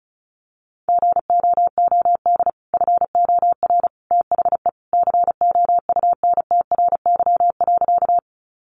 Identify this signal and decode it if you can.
Morse code, heard as a keyed tone, about 34 wpm, 700 Hz. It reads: GOOD FOR THE COUNTRY.
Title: Sentences from Top 300 Words 35wpm